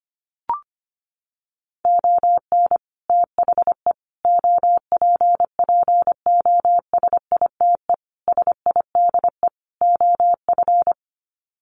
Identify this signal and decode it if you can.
Morse code, heard as a keyed tone, about 25 wpm, 700 Hz. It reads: ON THE OPPOSITE SIDE OF
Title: Sets of 5 Words Encore2 25wpm